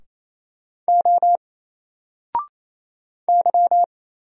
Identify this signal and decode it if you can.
Morse code, heard as a keyed tone, about 27 wpm, 700 Hz. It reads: O Y
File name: Single-Letter-Number-28wpm